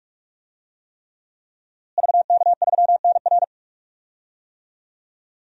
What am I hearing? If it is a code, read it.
Morse code, heard as a keyed tone, about 46 wpm, 700 Hz. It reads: VK3NR
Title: DXCC-100 Call Signs - 5 Characters - Repeated 3x - 45wpm